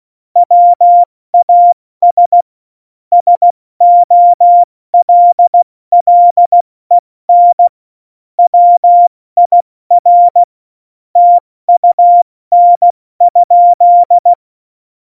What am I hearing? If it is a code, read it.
Morse code, heard as a keyed tone, about 16 wpm, 700 Hz. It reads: WAS SOLLEN WIR TUN?
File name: Morsekurs / Witze / Witz 16. Januar 2018, 70 F80